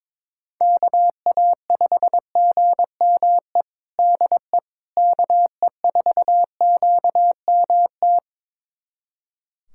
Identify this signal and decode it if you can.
Morse code, heard as a keyed tone, about 22 wpm, 700 Hz. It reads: KA5GME DE KE4QMT
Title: USA Callsign DE Callsign - 6 Characters - 22wpm